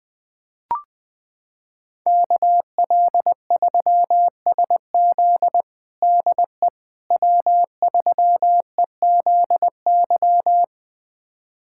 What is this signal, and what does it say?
Morse code, heard as a keyed tone, about 20 wpm, 700 Hz. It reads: KL3SZ DE W3EZY
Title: USA Callsign DE Callsign - 5 Characters - 20wpm